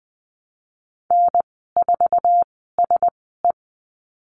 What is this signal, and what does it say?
Morse code, heard as a keyed tone, about 20 wpm, spaced as for 15 wpm, 700 Hz. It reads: N4SE
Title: 048 Lesson 9g - Number 4 - Call Signs 20wpm